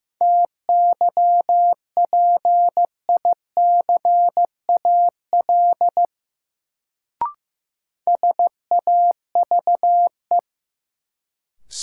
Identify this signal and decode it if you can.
Morse code, heard as a keyed tone, about 15 wpm, 700 Hz. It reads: TYPICAL SAVE
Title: Sentences from Top 1100 Words - New Word Drill Alt - 15wpm